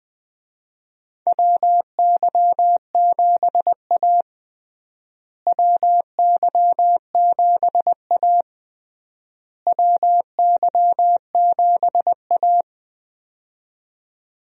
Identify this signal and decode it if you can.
Morse code, heard as a keyed tone, about 21 wpm, 700 Hz. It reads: WY7A WY7A WY7A
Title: USA Call Signs - 4 Characters - Repeated 3x - 20wpm